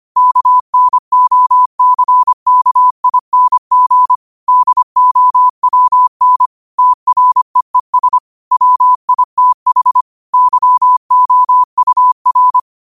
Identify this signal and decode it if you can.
Morse code, heard as a keyed tone, about 25 wpm, 1000 Hz. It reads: KNOCKING DOWN TREES WITH YOUR
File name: qotdfiles / 25wpm